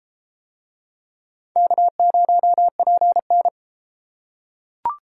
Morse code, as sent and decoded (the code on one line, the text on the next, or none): -.- ----- .--. -.
K0PN